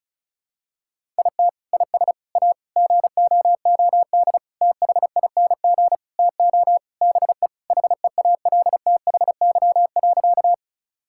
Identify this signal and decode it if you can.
Morse code, heard as a keyed tone, about 34 wpm, 700 Hz. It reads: IT IS A GOOD THING TO BE HEALTHY.
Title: Sentences from Top 1000 Words - Review Part 5 - 35wpm